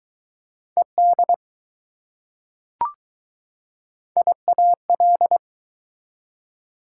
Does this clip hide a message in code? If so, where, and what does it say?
Yes: Morse code from 0.8 to 5.4 s, heard as a keyed tone, about 20 wpm, 700 Hz. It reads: ED IAL